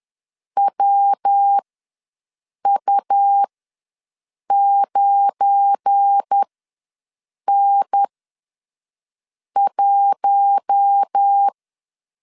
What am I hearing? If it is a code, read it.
Morse code, heard as a keyed tone, about 10 wpm, spaced as for 5 wpm, 800 Hz. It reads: WU9N1
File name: LEZIONI / 9 2867 ws 10wpm 4215